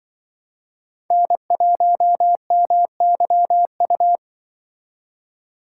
Morse code, heard as a keyed tone, about 24 wpm, 700 Hz. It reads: N1MYU
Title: Speed Racing - USA 1x3 Call Signs 20wpm